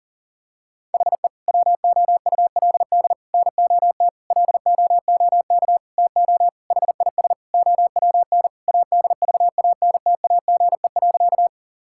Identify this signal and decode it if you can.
Morse code, heard as a keyed tone, about 40 wpm, 700 Hz. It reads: HE WOULD NOT LOOK TO HIS OWN ADVANTAGE.
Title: Sentences from Top 1000 Words - Review Part 6 - 40wpm